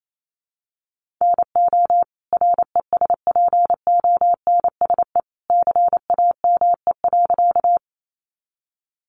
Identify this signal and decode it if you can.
Morse code, heard as a keyed tone, about 28 wpm, 700 Hz. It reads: NO RESPONSE CAME.
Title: Sentences from Top 1400 Words 28wpm